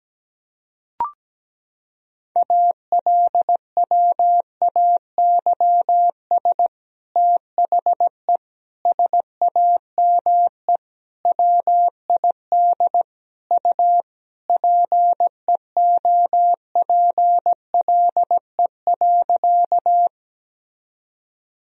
Morse code, heard as a keyed tone, about 17 wpm, 700 Hz. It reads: ALWAYS THE SAME WID U PEOPLE.
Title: Sentences from Top 300 Words and Abbreviations 17wpm